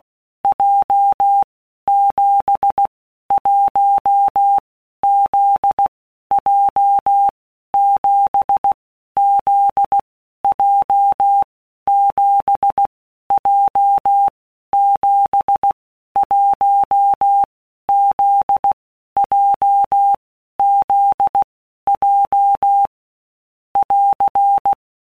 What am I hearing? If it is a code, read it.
Morse code, heard as a keyed tone, about 16 wpm, 800 Hz. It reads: J 7 1 Z J 7 Z J 7 J 7 1 Z J Z J <AR>
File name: Basic Letter Groups By Blocks / Groups@15 / JZ17@15